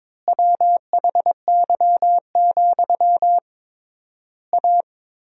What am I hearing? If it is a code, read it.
Morse code, heard as a keyed tone, about 22 wpm, 700 Hz. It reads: WHY, A
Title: 3x Word-Spacing - Sentences from Top 500 Words - 22wpm